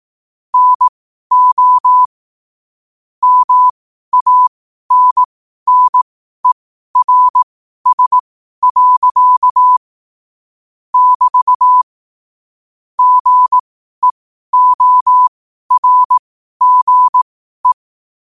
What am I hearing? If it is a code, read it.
Morse code, heard as a keyed tone, about 18 wpm, spaced as for 12 wpm, 1000 Hz. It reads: NO MANNERS. <BT> GEORGE
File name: qotdfiles / 12wpm